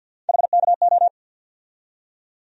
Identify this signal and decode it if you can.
Morse code, heard as a keyed tone, about 52 wpm, 700 Hz. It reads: HKO
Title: USA Call Sign Postfixes - 3 Letters 50wpm